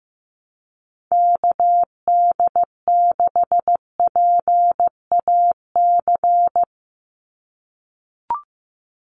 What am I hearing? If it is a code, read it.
Morse code, heard as a keyed tone, about 15 wpm, 700 Hz. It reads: KD6PAC
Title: USA Call Signs 15wpm